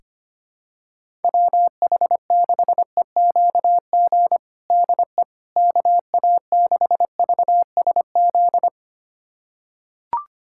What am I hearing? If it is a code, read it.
Morse code, heard as a keyed tone, about 25 wpm, 700 Hz. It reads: WH6EQG DE KA6VSZ
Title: USA Callsign DE Callsign - 6 Characters - 25wpm